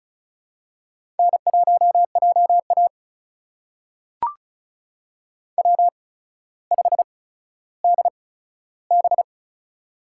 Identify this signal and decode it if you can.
Morse code, heard as a keyed tone, about 36 wpm, spaced as for 15 wpm, 700 Hz. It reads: N1JA W5DB
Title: USA Call Signs - 4 Characters - 8x Character Spacing - 35wpm